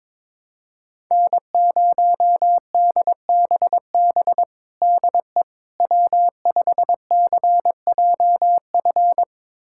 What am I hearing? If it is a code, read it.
Morse code, heard as a keyed tone, about 22 wpm, 700 Hz. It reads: N0DBB DE W5CJF
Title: USA Callsign DE Callsign - 5 Characters - 22wpm